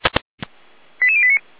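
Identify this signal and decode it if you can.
Five-tone selective call (ZVEI), heard as quick runs of five tones, tones of about 80 ms. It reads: 88089
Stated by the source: Received on an SDR as a ZVEI five-tone call.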